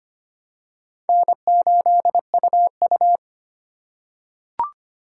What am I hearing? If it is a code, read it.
Morse code, heard as a keyed tone, about 25 wpm, 700 Hz. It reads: N8UU
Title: USA Call Signs - 4 Characters - 4x Character Spacing - 25wpm